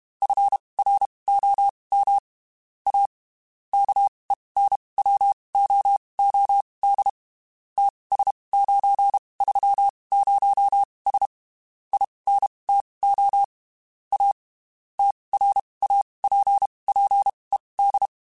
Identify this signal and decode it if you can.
Morse code, heard as a keyed tone, about 32 wpm, spaced as for 23 wpm, 800 Hz. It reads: FROM A KENWOOD TS930S INTO A TRAPPED
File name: (spaced) / QSOs Spaced-mp3s / kc7haa @30 sp